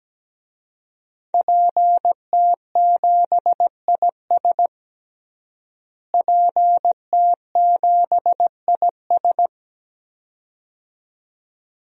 Morse code, heard as a keyed tone, about 17 wpm, 700 Hz. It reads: PT7IS PT7IS
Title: DXCC-100 Call Signs - 5 Characters - Repeated 2x - 17wpm